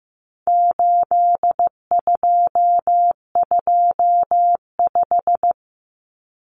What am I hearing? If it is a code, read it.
Morse code, heard as a keyed tone, about 15 wpm, 700 Hz. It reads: 8225
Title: Common Cut Numbers - 4-Digit 15wpm